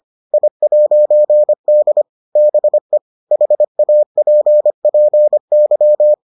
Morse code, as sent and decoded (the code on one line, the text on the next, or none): .. .----. -.. / -... / .... .- .--. .--. -.--
I'D BE HAPPY